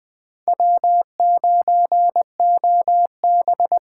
Morse code, heard as a keyed tone, about 20 wpm, 700 Hz. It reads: W9OB
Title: Speed Racing - USA 1x2 Call Signs 17wpm